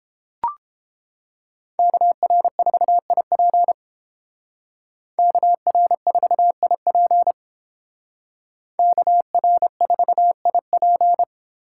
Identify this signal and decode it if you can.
Morse code, heard as a keyed tone, about 29 wpm, 700 Hz. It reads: KR4IP KR4IP KR4IP